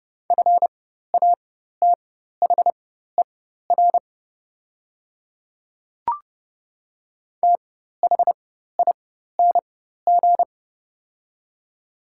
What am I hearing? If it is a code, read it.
Morse code, heard as a keyed tone, about 30 wpm, spaced as for 14 wpm, 700 Hz. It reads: FATHER THING
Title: 4x Character Spacing - Top 200 Words - 30wpm